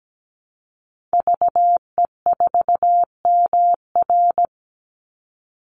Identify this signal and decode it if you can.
Morse code, heard as a keyed tone, about 17 wpm, 700 Hz. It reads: VE4MR
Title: Call Signs Repeated 2x - 17wpm